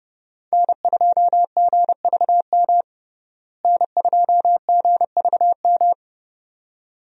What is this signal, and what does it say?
Morse code, heard as a keyed tone, about 32 wpm, 700 Hz. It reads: N2GVM N2GVM
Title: USA Call Signs - 5 Characters - Repeated 2x - 30wpm